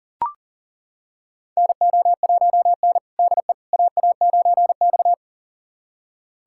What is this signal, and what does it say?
Morse code, heard as a keyed tone, about 40 wpm, 700 Hz. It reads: NO1N DE AA9X
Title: USA Callsign DE Callsign - 4 Characters - 40wpm